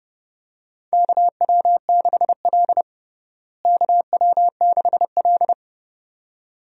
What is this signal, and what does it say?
Morse code, heard as a keyed tone, about 30 wpm, 700 Hz. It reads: KW6L KW6L
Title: USA Call Signs - 4 Characters - Repeated 2x - 30wpm